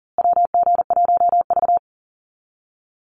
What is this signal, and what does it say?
Morse code, heard as a keyed tone, about 40 wpm, 700 Hz. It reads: WG1V
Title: USA Call Signs Repeated 3x 40wpm